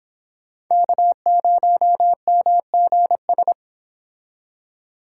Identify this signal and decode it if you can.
Morse code, heard as a keyed tone, about 26 wpm, 700 Hz. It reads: K0MGS